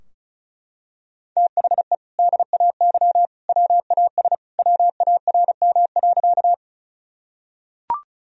Morse code, heard as a keyed tone, about 36 wpm, 700 Hz. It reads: THE DAY WAS WARM.
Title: Sentences from Top 1000 Words - Review Part 2 - 35wpm